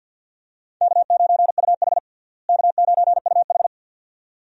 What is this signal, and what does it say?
Morse code, heard as a keyed tone, about 52 wpm, 700 Hz. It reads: K9UH K9UH
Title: USA Call Signs - 4 Characters - Repeated 2x - 50wpm